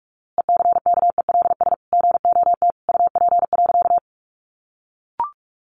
Morse code, heard as a keyed tone, about 44 wpm, 700 Hz. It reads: ECKELS GOT UP.
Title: A Sound Of Thunder 45wpm